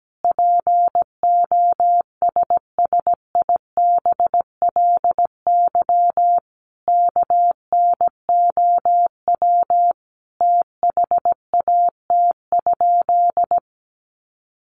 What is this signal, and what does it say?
Morse code, heard as a keyed tone, about 17 wpm, 700 Hz. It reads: POSSIBLY KNOW THAT?
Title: Sentences from Top 1300 Words 17wpm